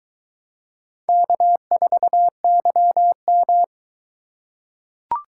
Morse code, heard as a keyed tone, about 22 wpm, 700 Hz. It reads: K4YM